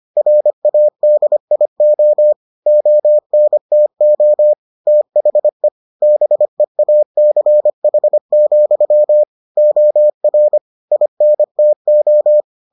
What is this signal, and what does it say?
Morse code, heard as a keyed tone, about 25 wpm, 600 Hz. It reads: RADIO ONTO THE BEACH, OR INTO